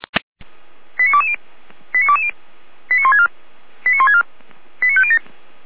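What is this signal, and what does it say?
Five-tone selective call (ZVEI), heard as quick runs of five tones, tones of about 70 ms. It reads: 89220 89220 89164 89164 89507
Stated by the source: ZVEI five-tone selective call 89172, received by SDR.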